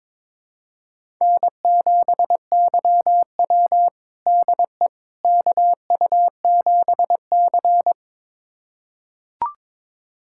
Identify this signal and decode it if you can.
Morse code, heard as a keyed tone, about 22 wpm, 700 Hz. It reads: N7YW DE KU7C
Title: USA Callsign DE Callsign - 4 Characters - 22wpm